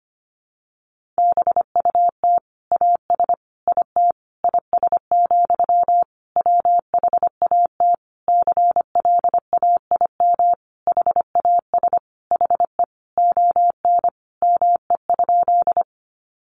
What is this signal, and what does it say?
Morse code, heard as a keyed tone, about 25 wpm, 700 Hz. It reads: BUT AS IT IS, WHAT CLAIM HAS HE ON ME?
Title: Sentences from Top 1500 Words 25wpm